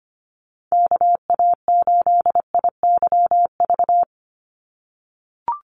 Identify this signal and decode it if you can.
Morse code, heard as a keyed tone, about 25 wpm, 700 Hz. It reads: KA8IYV
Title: USA Call Signs 25wpm